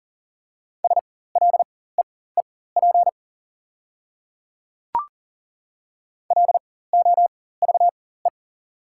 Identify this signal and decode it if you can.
Morse code, heard as a keyed tone, about 40 wpm, spaced as for 19 wpm, 700 Hz. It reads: SLEEP LOVE
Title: 4x Character Spacing - Top 500 Words - 40wpm